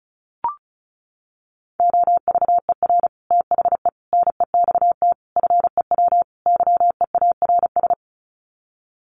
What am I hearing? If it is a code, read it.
Morse code, heard as a keyed tone, about 34 wpm, 700 Hz. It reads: OVER THE NEXT FEW YEARS
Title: Sets of 5 Words Encore 35wpm